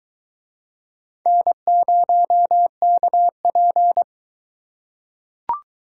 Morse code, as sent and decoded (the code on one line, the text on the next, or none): -. ----- -.- .--.
N0KP